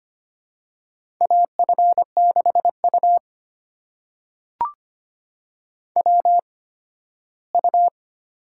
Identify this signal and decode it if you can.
Morse code, heard as a keyed tone, about 25 wpm, spaced as for 15 wpm, 700 Hz. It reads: AF6U WU
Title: USA Call Signs - 4 Characters - 8x Character Spacing - 25wpm